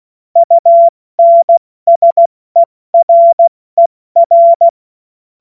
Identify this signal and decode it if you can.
Morse code, heard as a keyed tone, about 16 wpm, 680 Hz. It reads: UNSERER
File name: Morsekurs / Witze / Witz 06. September 2022 70 F80